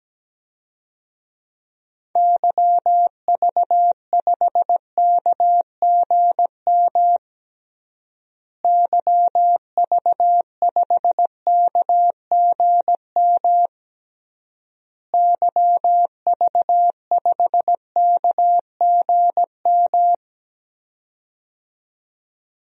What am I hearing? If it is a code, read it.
Morse code, heard as a keyed tone, about 17 wpm, 700 Hz. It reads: YV5KGM YV5KGM YV5KGM
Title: DXCC-100 Call Signs - 6 Characters - Repeated 3x - 17wpm